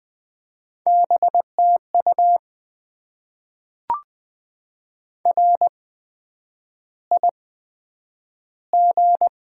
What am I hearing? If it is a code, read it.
Morse code, heard as a keyed tone, about 20 wpm, spaced as for 9 wpm, 700 Hz. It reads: BTU RIG
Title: Top 20 QSO Elements - 8x Character Spacing - 20wpm